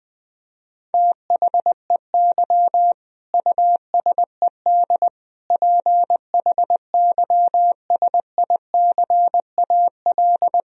Morse code, heard as a keyed tone, about 20 wpm, 700 Hz. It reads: THEY USED PHYSICAL
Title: Sentences from Top 1000 Words - Review Part 4 - 20wpm